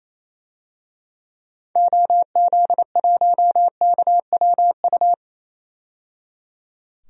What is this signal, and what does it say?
Morse code, heard as a keyed tone, about 28 wpm, 700 Hz. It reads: OZ1KWU
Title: DXCC-100 Call Signs - 6 Characters - 28wpm